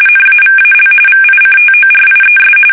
RTTY, about 45.45 baud. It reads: NAVAL VICTORIES
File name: fsk850 / usnavy2